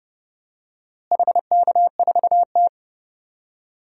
Morse code, heard as a keyed tone, about 30 wpm, 700 Hz. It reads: HK4T
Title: DXCC-100 Call Signs - 4 Characters - Repeated 2x - 30wpm